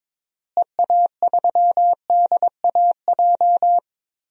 Morse code, heard as a keyed tone, about 22 wpm, 700 Hz. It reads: EA3DAJ
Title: DXCC-100 Call Signs - 6 Characters - Repeated 3x - 22wpm